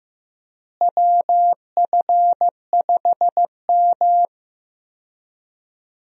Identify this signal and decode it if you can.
Morse code, heard as a keyed tone, about 15 wpm, 700 Hz. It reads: WF5M
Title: USA Call Signs - 4 Characters - Repeated 3x - 15wpm